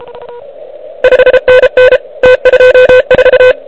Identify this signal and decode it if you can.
Morse code, heard as a keyed tone, about 27 wpm, 500 Hz. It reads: 5NN T24
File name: ARCHIV / QCX / QCX Pileup